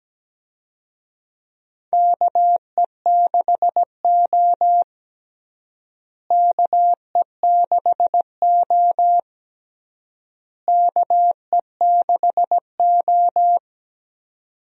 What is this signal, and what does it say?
Morse code, heard as a keyed tone, about 17 wpm, 700 Hz. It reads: KE6O KE6O KE6O
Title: USA Call Signs - 4 Characters - Repeated 3x - 17wpm